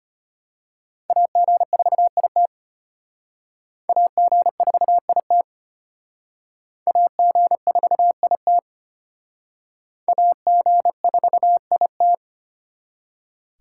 Morse code, heard as a keyed tone, about 32 wpm, 700 Hz. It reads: AG4IT AG4IT AG4IT AG4IT